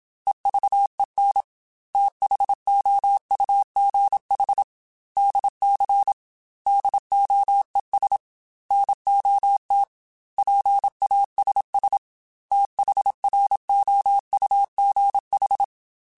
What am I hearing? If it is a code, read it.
Morse code, heard as a keyed tone, about 27 wpm, 800 Hz. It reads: EVEN THOUGH DC DOES NOT PASS THROUGH
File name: Technical Subjects / Technical Subjects-mp3s / Sw Pwr Sup @25